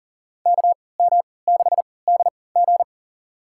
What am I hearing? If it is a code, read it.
Morse code, heard as a keyed tone, about 40 wpm, spaced as for 23 wpm, 700 Hz. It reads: KM6DG
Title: USA Call Signs - 5 Characters - 3x Character Spacing - 40wpm